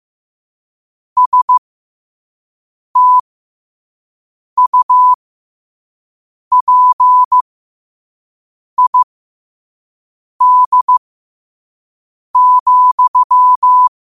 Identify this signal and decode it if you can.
Morse code, heard as a keyed tone, about 15 wpm, spaced as for 5 wpm, 1000 Hz. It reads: STUPID,
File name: qotdfiles / 5wpm